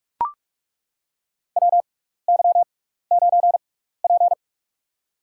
Morse code, heard as a keyed tone, about 42 wpm, spaced as for 15 wpm, 700 Hz. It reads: WY9P
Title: USA Call Signs - 6x Character Spacing - 45wpm